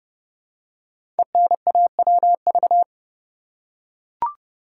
Morse code, heard as a keyed tone, about 30 wpm, 690 Hz. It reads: ENAWV
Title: Code Groups - 5 Letters - 30wpm